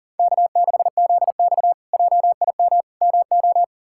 Morse code, heard as a keyed tone, about 37 wpm, 700 Hz. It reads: K6ZX JIM MO
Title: K1USN SST QSO Practice - 40wpm